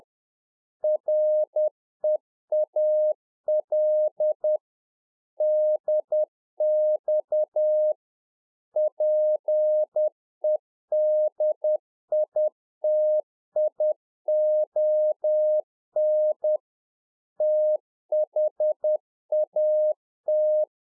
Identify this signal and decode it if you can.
Morse code, heard as a keyed tone, about 10 wpm, 610 Hz. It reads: REAL DX PEDITION THAT